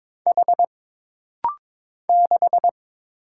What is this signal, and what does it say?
Morse code, heard as a keyed tone, about 22 wpm, 700 Hz. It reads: H 6
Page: This rapid-fire timing is geared to push your ICR (Instant Character Recognition) skills.